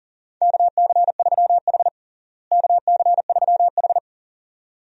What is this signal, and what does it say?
Morse code, heard as a keyed tone, about 40 wpm, 700 Hz. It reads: KC3H KC3H
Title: USA Call Signs - 4 Characters - Repeated 2x - 40wpm